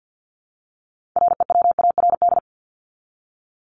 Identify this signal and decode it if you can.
Morse code, heard as a keyed tone, about 48 wpm, 700 Hz. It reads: REWARD
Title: Top 1751-2000 Words 50wpm